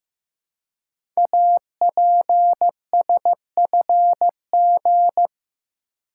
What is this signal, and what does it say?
Morse code, heard as a keyed tone, about 15 wpm, 700 Hz. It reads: APSFG
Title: Code Groups - 5 Letters - 15wpm